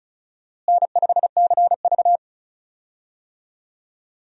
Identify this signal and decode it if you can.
Morse code, heard as a keyed tone, about 36 wpm, 700 Hz. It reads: N5CV
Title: USA Call Signs - 4 Characters - Repeated 3x - 35wpm